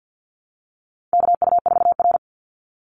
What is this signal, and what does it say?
Morse code, heard as a keyed tone, about 50 wpm, 700 Hz. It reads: KU4R
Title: NA SOTA Chaser Callsigns 50wpm